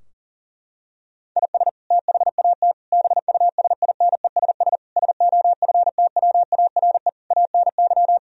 Morse code, heard as a keyed tone, about 40 wpm, 700 Hz. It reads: IS THAT BUSINESS SOFTWARE ANY
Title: Sentences from Top 1000 Words - Review Part 3 - 40wpm